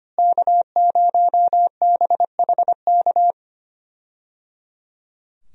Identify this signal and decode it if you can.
Morse code, heard as a keyed tone, about 25 wpm, 700 Hz. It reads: K0BHK
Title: USA Call Signs - 5 Characters - Repeated 3x - 25wpm